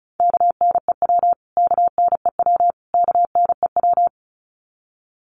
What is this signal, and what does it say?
Morse code, heard as a keyed tone, about 34 wpm, 700 Hz. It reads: KNEW KNEW KNEW
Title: Top 500 Words Repeated 35wpm